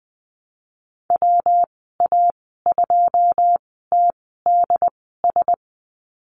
Morse code, heard as a keyed tone, about 20 wpm, spaced as for 15 wpm, 700 Hz. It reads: WA2TDS
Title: 128 Lesson 21g - Number 6 - Call Signs 20wpm